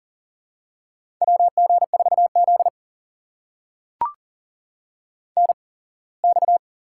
Morse code, heard as a keyed tone, about 40 wpm, spaced as for 24 wpm, 700 Hz. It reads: WG4Z NX
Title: USA Call Signs - 4 Characters - 8x Character Spacing - 40wpm